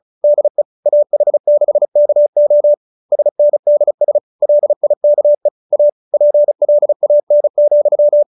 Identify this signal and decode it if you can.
Morse code, heard as a keyed tone, about 34 wpm, 590 Hz. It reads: DE AH6KO SNDS LIKE A PLAN,